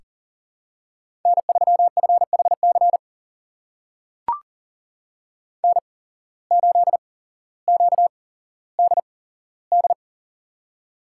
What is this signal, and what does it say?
Morse code, heard as a keyed tone, about 39 wpm, spaced as for 17 wpm, 700 Hz. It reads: N3FHC N8QDD
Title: USA Call Signs - 8x Character Spacing - 40wpm